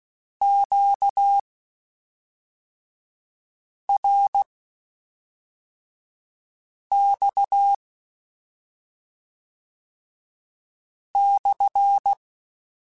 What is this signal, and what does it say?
Morse code, heard as a keyed tone, about 16 wpm, spaced as for 3 wpm, 800 Hz. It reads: QRX /
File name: Q Calls / Q Calls-mp3s / Q Calls SPSL2 @5